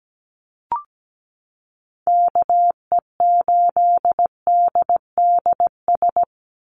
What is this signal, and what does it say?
Morse code, heard as a keyed tone, about 17 wpm, 700 Hz. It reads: KE8DDS
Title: USA Call Signs 17wpm